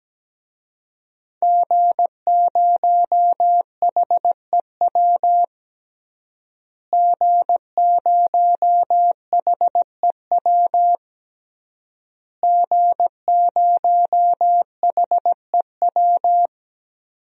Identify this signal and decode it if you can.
Morse code, heard as a keyed tone, about 17 wpm, 700 Hz. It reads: G0HEW G0HEW G0HEW
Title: DXCC-100 Call Signs - 5 Characters - Repeated 3x - 17wpm